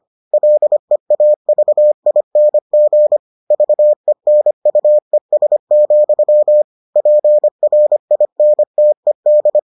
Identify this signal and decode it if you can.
Morse code, heard as a keyed tone, about 25 wpm, 600 Hz. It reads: LEAVING VENUES, PRINTED